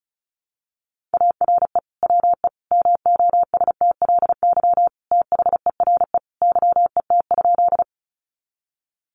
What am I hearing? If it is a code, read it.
Morse code, heard as a keyed tone, about 34 wpm, 700 Hz. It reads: ARE WE MOSTLY THERE YET?
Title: Sentences from Top 1300 Words 35wpm